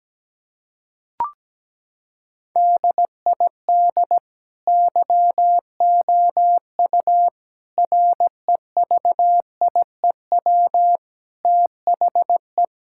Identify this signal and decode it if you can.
Morse code, heard as a keyed tone, about 17 wpm, 700 Hz. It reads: DID YOU REVIEW THE
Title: Sentences from Top 1000 Words - Review Part 6 - 17wpm